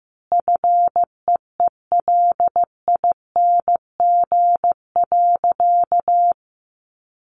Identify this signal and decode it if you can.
Morse code, heard as a keyed tone, about 15 wpm, 700 Hz. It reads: FEELING.
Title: Sentences from Top 1400 Words 15wpm